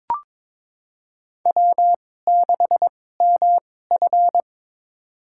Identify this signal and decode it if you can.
Morse code, heard as a keyed tone, about 22 wpm, spaced as for 16 wpm, 700 Hz. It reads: W6MF
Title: USA Call Signs - 4 Characters - 2x Character Spacing - 22wpm